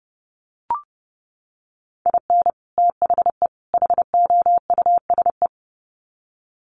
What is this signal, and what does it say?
Morse code, heard as a keyed tone, about 30 wpm, 700 Hz. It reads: IN THE HOUSE
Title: Sets of 3 Words 30wpm